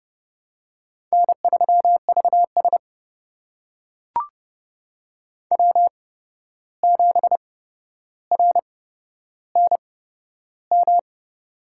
Morse code, heard as a keyed tone, about 30 wpm, spaced as for 12 wpm, 700 Hz. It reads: N3VS W7RNM
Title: USA Call Signs - 8x Character Spacing - 30wpm